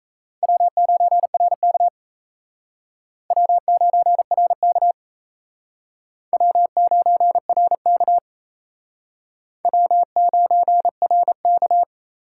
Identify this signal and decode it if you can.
Morse code, heard as a keyed tone, about 38 wpm, 700 Hz. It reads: W9RK W9RK W9RK W9RK